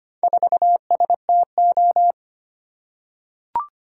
Morse code, heard as a keyed tone, about 25 wpm, 700 Hz. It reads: 4STO
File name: practice / 25wpm / Four Letter-Number Combinations - 25wpm